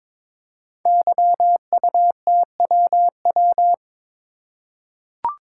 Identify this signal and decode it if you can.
Morse code, heard as a keyed tone, about 22 wpm, 690 Hz. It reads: YUTWW
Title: Code Groups - 5 Letters - 22wpm